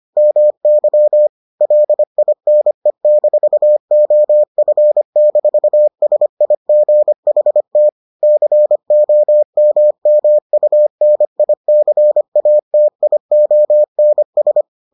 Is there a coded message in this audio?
Yes: Morse code from 0.2 to 14.6 s, heard as a keyed tone, about 25 wpm, 600 Hz. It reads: MY LINE-OF-SIGHT COMMUNICATIONS